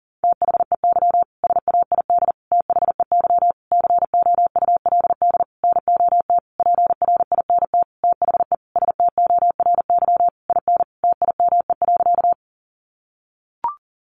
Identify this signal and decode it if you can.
Morse code, heard as a keyed tone, about 40 wpm, 700 Hz. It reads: THEY SAID THEY COULD NOT PRINT THE STORY IN TIME.
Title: Sentences from Top 1400 Words 40wpm